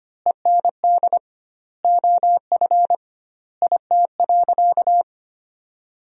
Morse code, heard as a keyed tone, about 25 wpm, 700 Hz. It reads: END OF IT.
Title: 2x Word-Spacing - Sentences from Top 200 Words - 25wpm